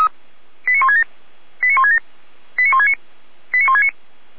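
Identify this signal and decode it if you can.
Five-tone selective call (ZVEI), heard as quick runs of five tones, tones of about 80 ms. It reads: 89167 89167 89169 89169
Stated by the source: Five-tone ZVEI selective call received on an SDR, channel 455, archived by Feuerwehr Orlamünde.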